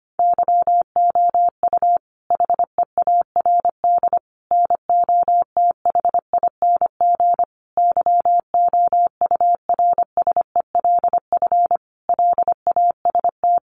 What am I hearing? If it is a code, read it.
Morse code, heard as a keyed tone, about 25 wpm, 700 Hz. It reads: YOU HEARD NOTHING YOURSELF LAST
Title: Sentences from Top 600 Words 25wpm